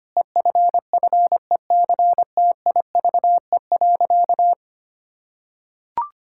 Morse code, heard as a keyed tone, about 25 wpm, 700 Hz. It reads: EFFECTIVE.
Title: Sentences from Top 1000 Words - Review Part 6 - 25wpm